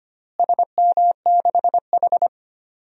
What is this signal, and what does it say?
Morse code, heard as a keyed tone, about 25 wpm, 700 Hz. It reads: SM6H
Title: DXCC-100 Call Signs - 4 Characters - Repeated 2x - 25wpm